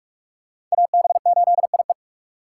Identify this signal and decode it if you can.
Morse code, heard as a keyed tone, about 42 wpm, 700 Hz. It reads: AB8IE